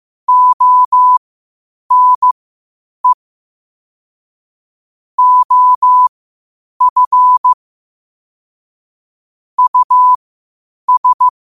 Morse code, heard as a keyed tone, about 15 wpm, spaced as for 8 wpm, 1000 Hz. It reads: ONE OF US
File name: qotdfiles / 8wpm